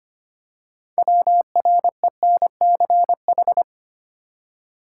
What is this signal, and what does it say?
Morse code, heard as a keyed tone, about 25 wpm, 700 Hz. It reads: WRENCH
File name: practice / 25wpm / Easy Words with Hints - 25wpm